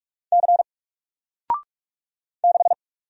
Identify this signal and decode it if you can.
Morse code, heard as a keyed tone, about 46 wpm, 700 Hz. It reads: C 6
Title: Single Character (Letter:Number:Punctuation:Prosign) 45wpm